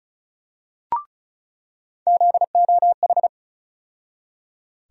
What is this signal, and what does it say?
Morse code, heard as a keyed tone, about 36 wpm, 700 Hz. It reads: ZOH
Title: USA Call Sign Postfixes - 3 Letters 35wpm